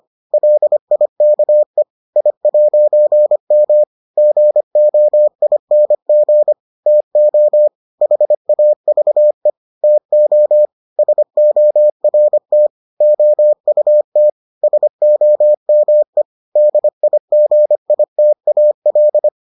Morse code, heard as a keyed tone, about 25 wpm, 600 Hz. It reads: LIKE I'M GOING TO HAVE TO SORT OUT SOME DIGITAL